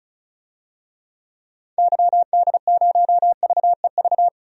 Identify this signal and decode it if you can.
Morse code, heard as a keyed tone, about 34 wpm, 700 Hz. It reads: YD0VEV
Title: DXCC-100 Call Signs - 6 Characters - 35wpm